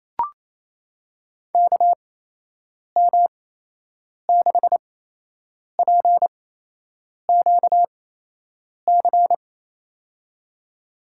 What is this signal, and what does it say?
Morse code, heard as a keyed tone, about 27 wpm, spaced as for 8 wpm, 700 Hz. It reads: KM6PQC
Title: USA Call Signs - 6 Characters - 8x Character Spacing - 28wpm